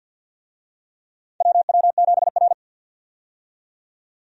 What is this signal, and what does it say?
Morse code, heard as a keyed tone, about 51 wpm, 700 Hz. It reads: WW7R
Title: USA Call Signs - 4 Characters - 50wpm